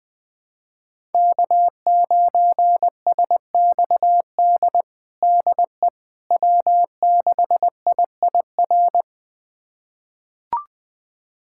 Morse code, heard as a keyed tone, about 20 wpm, 700 Hz. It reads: K9SXD DE W6IIR
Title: USA Callsign DE Callsign - 5 Characters - 20wpm